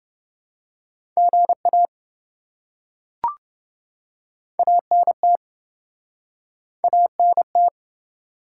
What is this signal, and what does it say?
Morse code, heard as a keyed tone, about 27 wpm, 700 Hz. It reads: GA ANT ANT